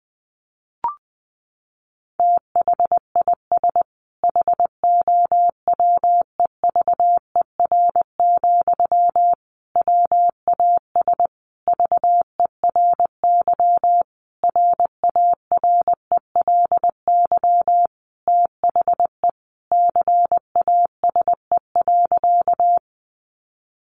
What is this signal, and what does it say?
Morse code, heard as a keyed tone, about 20 wpm, 700 Hz. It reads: THIS HOWEVER, WAS VERY RARELY THE CASE.
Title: Sentences from Top 1400 Words 20wpm